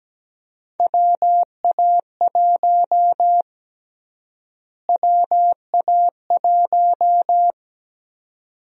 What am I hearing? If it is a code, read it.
Morse code, heard as a keyed tone, about 17 wpm, 700 Hz. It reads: WA1 WA1
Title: USA Call Sign Prefixes - Repeated 2x - 17wpm